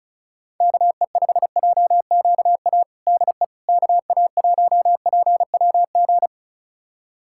Morse code, heard as a keyed tone, about 36 wpm, 700 Hz. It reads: KE5JQA DE KA1PWG
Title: USA Callsign DE Callsign 35wpm